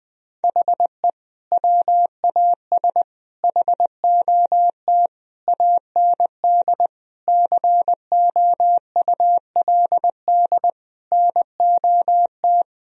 Morse code, heard as a keyed tone, about 21 wpm, 700 Hz. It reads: HE WAS HOT AND COULD NOT
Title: Sentences from Top 1000 Words - Review Part 3 - 20wpm